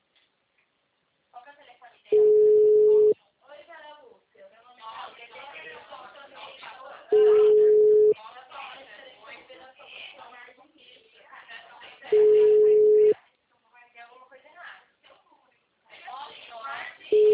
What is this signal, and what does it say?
Ringback tone (Europe, 425 Hz), with speech.
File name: out- -2004- - - .21285